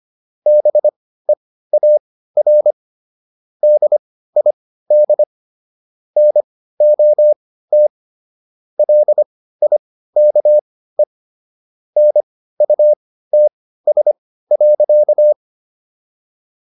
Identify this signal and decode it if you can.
Morse code, heard as a keyed tone, about 25 wpm, spaced as for 15 wpm, 600 Hz. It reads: BEAR DID NOT LIKE NUTS.